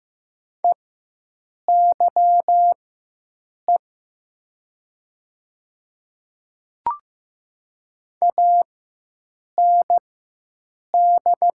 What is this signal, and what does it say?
Morse code, heard as a keyed tone, about 15 wpm, spaced as for 7 wpm, 700 Hz. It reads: EYE AND
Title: 4x Character Spacing - Top 300 Words - 15wpm